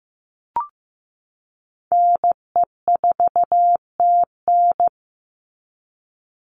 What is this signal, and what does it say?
Morse code, heard as a keyed tone, about 15 wpm, 700 Hz. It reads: NE4TN
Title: NA SOTA Chaser Callsigns 15wpm